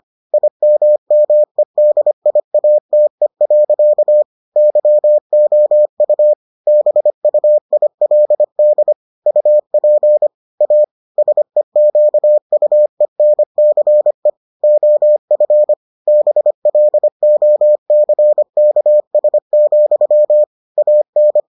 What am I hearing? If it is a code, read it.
Morse code, heard as a keyed tone, about 25 wpm, 600 Hz. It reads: IMMEDIATE. YOU BUILD UP A SEQUENCE OF BLOCKS, AN